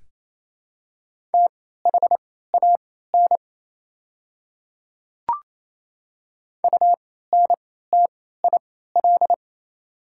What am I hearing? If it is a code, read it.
Morse code, heard as a keyed tone, about 28 wpm, spaced as for 16 wpm, 700 Hz. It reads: THAN UNTIL
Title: 3x Character Spacing - Top 300 Words - 28wpm